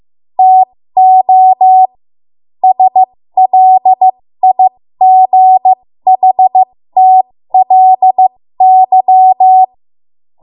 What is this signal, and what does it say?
Morse code, heard as a keyed tone, about 15 wpm, spaced as for 13 wpm, 750 Hz. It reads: TO SLIGHTLY
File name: Morse / 13WPM